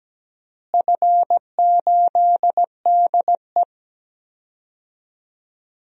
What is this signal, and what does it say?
Morse code, heard as a keyed tone, about 17 wpm, 700 Hz. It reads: F8DE
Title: DXCC-100 Call Signs - 4 Characters - Repeated 3x - 17wpm